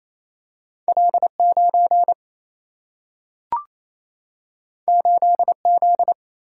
Morse code, heard as a keyed tone, about 28 wpm, 700 Hz. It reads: L9 8Z
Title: Single-Letter Single-Number 28wpm